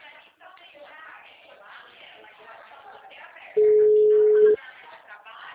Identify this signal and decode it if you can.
Ringback tone (Europe, 425 Hz), with speech.